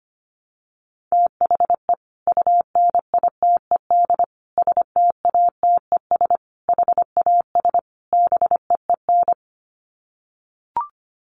Morse code, heard as a keyed tone, about 25 wpm, 700 Hz. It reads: THE UNITED STATES HAS BEEN
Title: Sets of 5 Words Encore2 25wpm